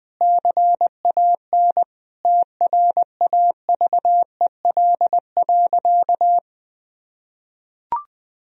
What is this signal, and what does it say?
Morse code, heard as a keyed tone, about 20 wpm, 700 Hz. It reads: CAN TRAVEL.
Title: Sentences from Top 1000 Words - Review Part 2 - 20wpm